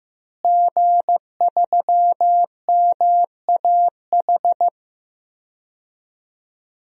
Morse code, heard as a keyed tone, about 16 wpm, 700 Hz. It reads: G3MAH
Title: DXCC-100 Call Signs - 5 Characters - Repeated 2x - 15wpm